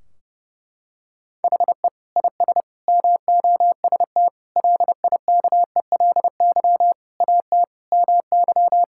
Morse code, heard as a keyed tone, about 28 wpm, 700 Hz. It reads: HE IS MOST LIKELY AT MY
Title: Sentences from Top 1600 Words 30wpm